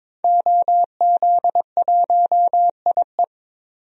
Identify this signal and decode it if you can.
Morse code, heard as a keyed tone, about 22 wpm, 700 Hz. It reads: OZ1IE
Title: DXCC-100 Call Signs - 5 Characters - 22wpm